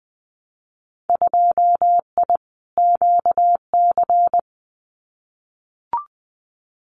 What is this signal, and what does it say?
Morse code, heard as a keyed tone, about 20 wpm, 700 Hz. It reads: 2I QC
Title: Winter Field Day Exchanges 20wpm